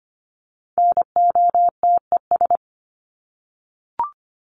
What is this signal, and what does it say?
Morse code, heard as a keyed tone, about 25 wpm, 700 Hz. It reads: NOTES
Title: ES:S Suffix - Top 100 Words with Base Words 25wpm